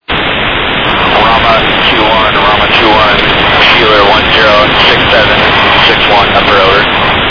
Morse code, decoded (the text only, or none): none